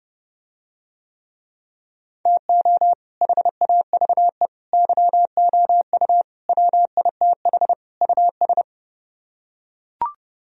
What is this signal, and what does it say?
Morse code, heard as a keyed tone, about 31 wpm, 700 Hz. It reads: TO HAVE YOU WITH US